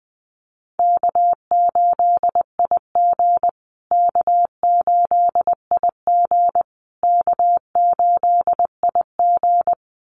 Morse code, heard as a keyed tone, about 20 wpm, 700 Hz. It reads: K8IG K8IG K8IG
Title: USA Call Signs Repeated 3x 20wpm